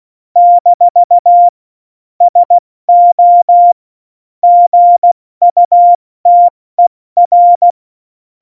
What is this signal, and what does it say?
Morse code, heard as a keyed tone, about 16 wpm, 700 Hz. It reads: - SO GUTER